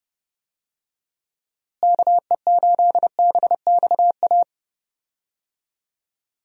Morse code, heard as a keyed tone, about 30 wpm, 700 Hz. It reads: KE8BXA
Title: USA Call Signs - 6 Characters - Repeated 3x - 30wpm